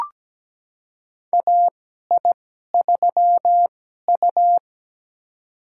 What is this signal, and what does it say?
Morse code, heard as a keyed tone, about 17 wpm, spaced as for 12 wpm, 700 Hz. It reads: AI3U
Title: USA Call Signs - 4 Characters - 2x Character Spacing - 17wpm